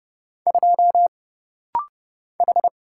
Morse code, heard as a keyed tone, about 30 wpm, 700 Hz. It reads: 2 H
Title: Single Letter-Number - Rapid-Fire 30wpm